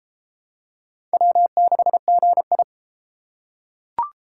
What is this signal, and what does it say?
Morse code, heard as a keyed tone, about 30 wpm, 700 Hz. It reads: W6GI